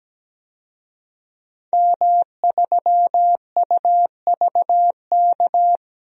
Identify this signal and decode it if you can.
Morse code, heard as a keyed tone, about 17 wpm, 700 Hz. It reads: M3UVK
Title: DXCC-100 Call Signs - 5 Characters - 17wpm